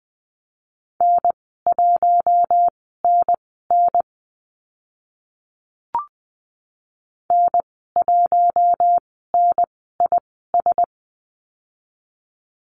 Morse code, heard as a keyed tone, about 20 wpm, spaced as for 14 wpm, 700 Hz. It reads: N1NN N1NIS
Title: 042 Lesson 8g - Number 1 - Call Signs 20wpm